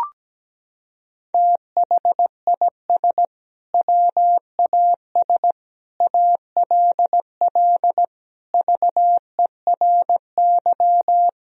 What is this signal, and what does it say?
Morse code, heard as a keyed tone, about 17 wpm, 700 Hz. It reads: THIS WAS ALL VERY
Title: Sentences from Top 1000 Words - Review Part 3 - 17wpm